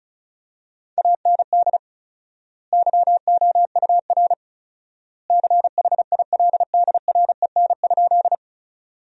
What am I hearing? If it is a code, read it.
Morse code, heard as a keyed tone, about 34 wpm, 700 Hz. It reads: AND YOUR CHILDREN?
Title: 4x Word-Spacing - Sentences from Top 300 Words - 35wpm